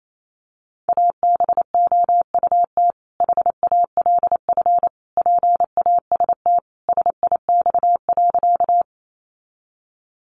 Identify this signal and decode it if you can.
Morse code, heard as a keyed tone, about 28 wpm, 700 Hz. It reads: ABOUT HALF PAST SIX.
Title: Sentences from Top 600 Words 28wpm